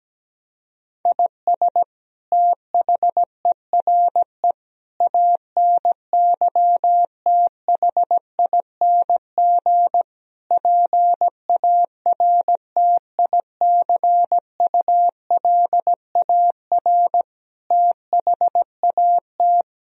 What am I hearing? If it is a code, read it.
Morse code, heard as a keyed tone, about 17 wpm, 700 Hz. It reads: IS THERE ANYTHING PARTICULAR THAT
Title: Sentences from Top 1000 Words - Review Part 4 - 17wpm